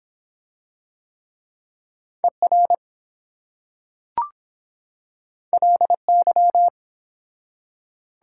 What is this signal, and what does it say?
Morse code, heard as a keyed tone, about 23 wpm, 700 Hz. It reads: ER LY